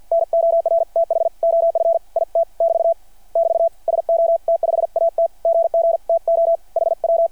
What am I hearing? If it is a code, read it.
Morse code, heard as a keyed tone, about 53 wpm, 660 Hz. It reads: NORTH OF IT <BT> <BT> SO T5AT GG TO HW
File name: archived / CRSnet